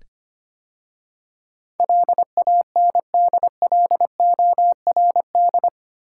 Morse code, heard as a keyed tone, about 25 wpm, 700 Hz. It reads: LANDLORD
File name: 8 Letter Words - First 3 Letters Given - 25wpm